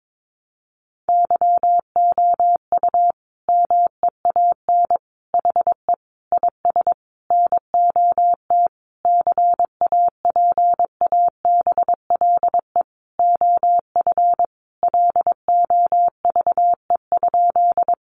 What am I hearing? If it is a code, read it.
Morse code, heard as a keyed tone, about 22 wpm, 700 Hz. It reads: YOU MEAN HE IS NOT CAPABLE OF LOVE?
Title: Sentences from Top 1200 Words 22wpm